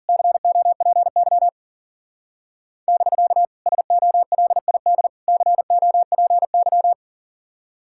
Morse code, heard as a keyed tone, about 49 wpm, 700 Hz. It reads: COPY <BK> SOLID COPY
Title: Speed-Racing Common QSO Phrases Encore - 40wpm